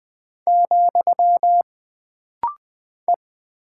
Morse code, heard as a keyed tone, about 20 wpm, 700 Hz. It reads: , E
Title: Single Character (Letter:Number:Punctuation:Prosign) - Rapid-Fire - 20wpm